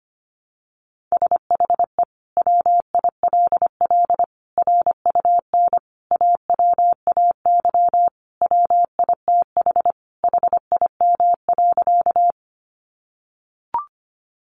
Morse code, heard as a keyed tone, about 25 wpm, 700 Hz. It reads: SHE WILL RUN AWAY WITH HIM.
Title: Sentences from Top 600 Words 25wpm